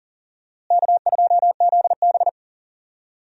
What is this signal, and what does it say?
Morse code, heard as a keyed tone, about 40 wpm, 700 Hz. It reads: K2ZB
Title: USA Call Signs - 4 Characters - Repeated 3x - 40wpm